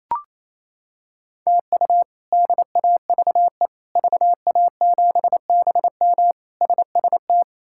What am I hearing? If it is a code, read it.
Morse code, heard as a keyed tone, about 28 wpm, 700 Hz. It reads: TU DAVE VA7BM SST
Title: K1USN SST QSO Practice - 28wpm